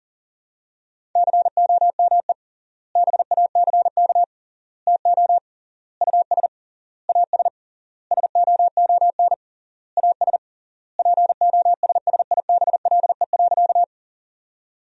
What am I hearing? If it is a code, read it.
Morse code, heard as a keyed tone, about 40 wpm, 700 Hz. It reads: COME BACK TO US AS SOON AS POSSIBLE.
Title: 3x Word-Spacing - Sentences from Top 500 Words - 40wpm